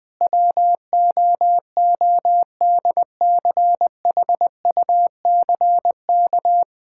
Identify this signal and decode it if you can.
Morse code, heard as a keyed tone, about 20 wpm, 700 Hz. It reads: WOODCHUCK
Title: Pangram Challenge v2 - 20wpm